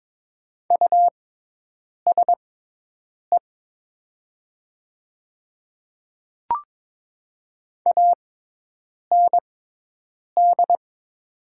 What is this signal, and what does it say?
Morse code, heard as a keyed tone, about 22 wpm, spaced as for 8 wpm, 700 Hz. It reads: USE AND
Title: 6x Character Spacing - Top 100 Words - 22wpm